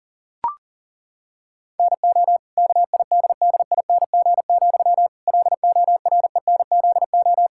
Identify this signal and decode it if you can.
Morse code, heard as a keyed tone, about 40 wpm, 700 Hz. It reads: NO KIDDING, LORENZO
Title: Pangram Challenge v2 - 40wpm